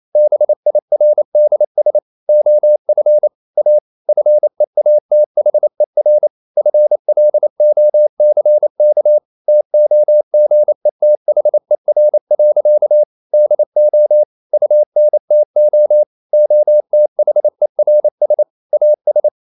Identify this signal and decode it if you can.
Morse code, heard as a keyed tone, about 28 wpm, 600 Hz. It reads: BIRDS OF A FEATHER FLOCK TOGETHER. DO UNTO OTHERS AS
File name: CW / Practice / EnglishSayings28